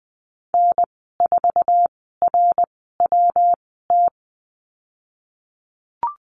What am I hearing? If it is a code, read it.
Morse code, heard as a keyed tone, about 20 wpm, spaced as for 15 wpm, 700 Hz. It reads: N4RWT
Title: 147 Lesson 24g - Letter y - Call Signs 20wpm